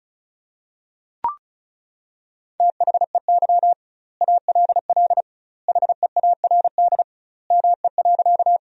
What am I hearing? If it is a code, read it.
Morse code, heard as a keyed tone, about 34 wpm, 700 Hz. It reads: THEY ALL HEARD ME.
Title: 2x Word-Spacing - Sentences from Top 400 Words - 35wpm